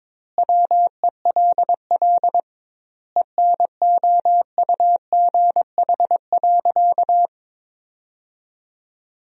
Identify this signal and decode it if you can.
Morse code, heard as a keyed tone, about 22 wpm, 700 Hz. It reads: WELL ENOUGH.
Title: 2x Word-Spacing - Sentences from Top 400 Words - 22wpm